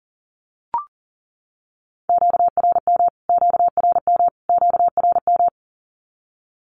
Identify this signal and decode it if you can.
Morse code, heard as a keyed tone, about 40 wpm, 700 Hz. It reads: QRM QRM QRM
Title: Q-Codes Repeated 3x 40wpm